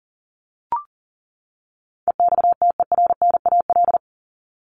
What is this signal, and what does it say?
Morse code, heard as a keyed tone, about 40 wpm, 700 Hz. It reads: EXTERNAL
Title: Top 1751-2000 Words 40wpm